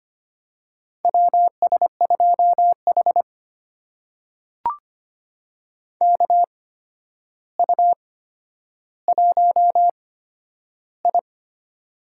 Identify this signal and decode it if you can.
Morse code, heard as a keyed tone, about 25 wpm, spaced as for 11 wpm, 700 Hz. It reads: WS2H KU1I
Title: USA Call Signs - 4 Characters - 8x Character Spacing - 25wpm